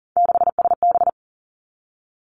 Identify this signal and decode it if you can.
Morse code, heard as a keyed tone, about 40 wpm, 700 Hz. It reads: 6SB
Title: Three Challenging Letter-Number Combinations 40wpm